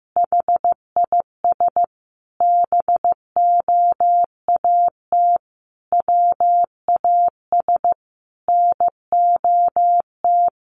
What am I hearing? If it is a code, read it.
Morse code, heard as a keyed tone, about 14 wpm, 700 Hz. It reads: HIS BOAT WAS NOT
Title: Sentences from Top 1400 Words 15wpm